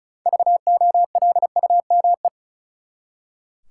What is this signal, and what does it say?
Morse code, heard as a keyed tone, about 36 wpm, 700 Hz. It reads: VOLUME
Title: Six Letter Words - 35wpm